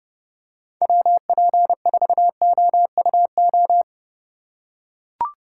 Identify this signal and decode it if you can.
Morse code, heard as a keyed tone, about 30 wpm, 700 Hz. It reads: WP4OUO